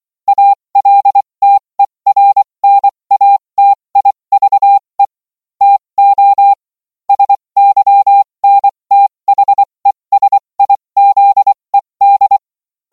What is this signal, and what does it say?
Morse code, heard as a keyed tone, about 24 wpm, spaced as for 20 wpm, 800 Hz. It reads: ALTERNATIVE TO SYNTHESIZED